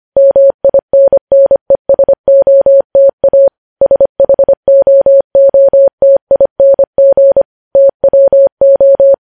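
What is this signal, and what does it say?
Morse code, heard as a keyed tone, about 25 wpm, 550 Hz. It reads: MINNESOTA SHOOTING TWO
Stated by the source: Morse code news headlines sent at 25 wpm.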